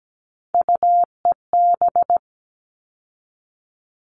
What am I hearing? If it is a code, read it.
Morse code, heard as a keyed tone, about 17 wpm, 700 Hz. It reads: UEB
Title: All Cut Numbers - 3-Digit 17wpm